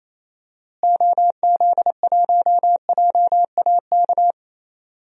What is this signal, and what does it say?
Morse code, heard as a keyed tone, about 28 wpm, 700 Hz. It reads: OZ1JAK
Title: DXCC-100 Call Signs - 6 Characters - Repeated 2x - 28wpm